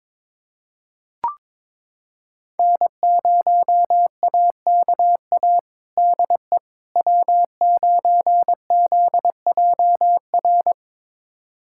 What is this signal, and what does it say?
Morse code, heard as a keyed tone, about 22 wpm, 700 Hz. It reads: N0AKA DE W9ZJR
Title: USA Callsign DE Callsign - 5 Characters - 22wpm